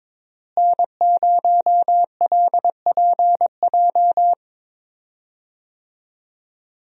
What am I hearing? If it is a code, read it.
Morse code, heard as a keyed tone, about 22 wpm, 700 Hz. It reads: N0LPJ
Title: USA Call Signs - 5 Characters - Repeated 2x - 22wpm